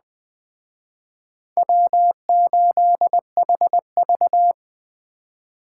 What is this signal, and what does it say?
Morse code, heard as a keyed tone, about 20 wpm, 700 Hz. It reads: W8HV